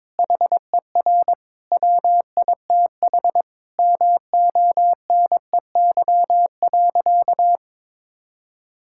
Morse code, heard as a keyed tone, about 21 wpm, 700 Hz. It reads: HER WITH MONEY.
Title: Sentences from Top 1600 Words 22wpm